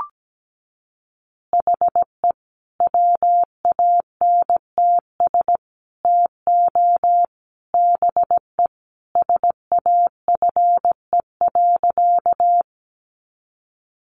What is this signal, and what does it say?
Morse code, heard as a keyed tone, about 17 wpm, 700 Hz. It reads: HE WANTS TO BE SAFE.
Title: Additional Easy Phrases 17wpm